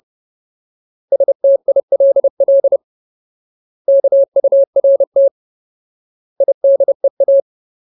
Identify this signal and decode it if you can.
Morse code, heard as a keyed tone, about 30 wpm, 550 Hz. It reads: STILL KURT IDEA
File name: Ultimate Word List / 30WPM Ultimate